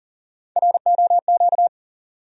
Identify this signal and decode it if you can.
Morse code, heard as a keyed tone, about 40 wpm, 700 Hz. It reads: ROQ
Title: USA Call Sign Postfixes - 3 Letters 40wpm